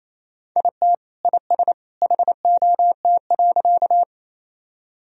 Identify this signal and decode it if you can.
Morse code, heard as a keyed tone, about 28 wpm, 700 Hz. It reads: IT IS HOT.
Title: Sentences from Top 1000 Words - Review Part 3 - 28wpm